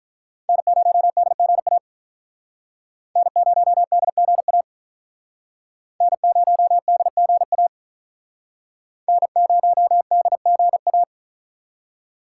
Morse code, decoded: N0DGA N0DGA N0DGA N0DGA

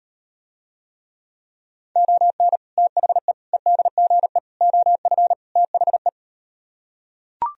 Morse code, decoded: ON THE EDGE OF THE

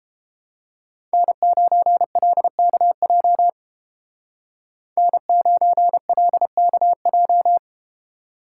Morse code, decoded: N9LKJ N9LKJ

